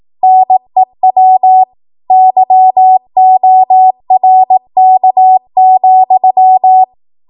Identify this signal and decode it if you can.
Morse code, heard as a keyed tone, about 18 wpm, 750 Hz. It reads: NEW YORK,
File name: Archive / 18 WPM / 18WPM